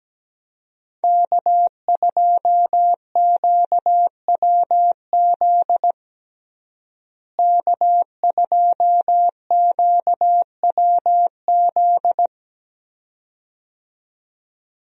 Morse code, heard as a keyed tone, about 17 wpm, 700 Hz. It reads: K2QWZ K2QWZ
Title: USA Call Signs - 5 Characters - Repeated 2x - 17wpm